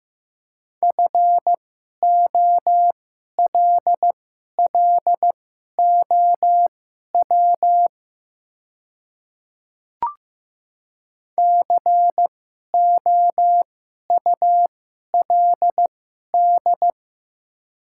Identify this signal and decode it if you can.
Morse code, heard as a keyed tone, about 15 wpm, spaced as for 11 wpm, 700 Hz. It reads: FOLLOW COULD
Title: 2x Character Spacing - Top 200 Words - 15wpm